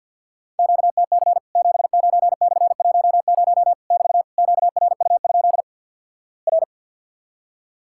Morse code, heard as a keyed tone, about 48 wpm, 700 Hz. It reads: QTC 79/10 <BT> QRU? R
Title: Worked All Europe DX Contest - QTC Exchange with Repeats - 50wpm